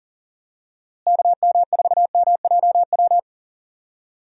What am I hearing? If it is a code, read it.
Morse code, heard as a keyed tone, about 40 wpm, 700 Hz. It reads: KM4MJW
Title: USA Call Signs - 6 Characters - Repeated 3x - 40wpm